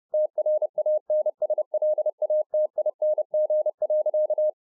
Morse code, heard as a keyed tone, about 30 wpm, 610 Hz. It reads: TRANSLATING.